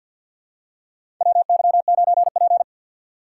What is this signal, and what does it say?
Morse code, heard as a keyed tone, about 50 wpm, 700 Hz. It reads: WY9P
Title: USA Call Signs - 4 Characters - 6x Character Spacing - 50wpm